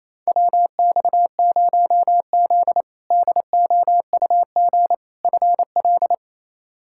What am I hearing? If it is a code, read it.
Morse code, heard as a keyed tone, about 28 wpm, 700 Hz. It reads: WX0Z DOUG FL
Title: K1USN SST QSO Practice - 28wpm